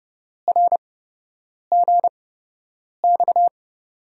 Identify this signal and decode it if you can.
Morse code, heard as a keyed tone, about 30 wpm, spaced as for 8 wpm, 700 Hz. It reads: RGX
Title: USA Call Signs - 6 Characters - 8x Character Spacing - 30wpm